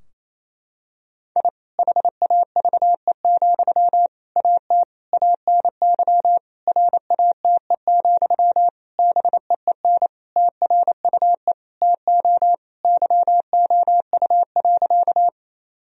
Morse code, heard as a keyed tone, about 28 wpm, 700 Hz. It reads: I HAVE, AT ANY RATE, BEEN TRUE TO YOU.
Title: Sentences from Top 1000 Words - Review Part 4 - 28wpm